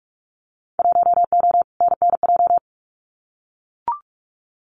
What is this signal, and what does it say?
Morse code, heard as a keyed tone, about 44 wpm, 700 Hz. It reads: 1O NNJ
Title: Winter Field Day Exchanges 45wpm